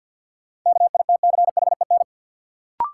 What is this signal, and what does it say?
Morse code, heard as a keyed tone, about 50 wpm, 700 Hz. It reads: KITCHEN
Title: Sentences from Top 1400 Words - New Word Drill - 50wpm